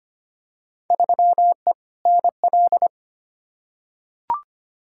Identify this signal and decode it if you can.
Morse code, heard as a keyed tone, about 25 wpm, 700 Hz. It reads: 3E NL
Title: Field Day Exchanges Encore - 25wpm